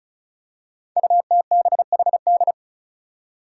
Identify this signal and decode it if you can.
Morse code, heard as a keyed tone, about 36 wpm, 700 Hz. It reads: UTBHD
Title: Five Challenging Letter-Number Combinations 35wpm